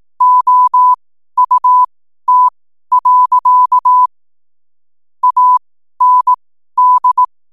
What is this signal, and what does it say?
Morse code, heard as a keyed tone, about 18 wpm, spaced as for 12 wpm, 1000 Hz. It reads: OUT. AND
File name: n7net / 12wpm